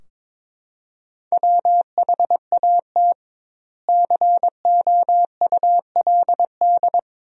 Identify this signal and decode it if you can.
Morse code, heard as a keyed tone, about 22 wpm, 700 Hz. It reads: WHAT COULD
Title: 2x Word-Spacing - Sentences from Top 200 Words - 22wpm